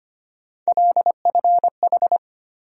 Morse code, heard as a keyed tone, about 25 wpm, 700 Hz. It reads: LFH
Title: USA Call Sign Postfixes - 3 Letters 25wpm